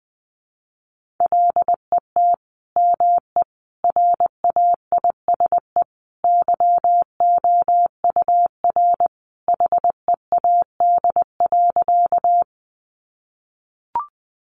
Morse code, heard as a keyed tone, about 20 wpm, 700 Hz. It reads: LET ME RAISE YOUR HEAD.
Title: Sentences from Top 1400 Words 20wpm